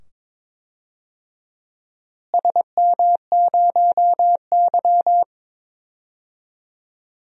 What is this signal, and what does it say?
Morse code, heard as a keyed tone, about 22 wpm, 700 Hz. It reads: SM0Y
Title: DXCC-100 Call Signs - 4 Characters - Repeated 2x - 22wpm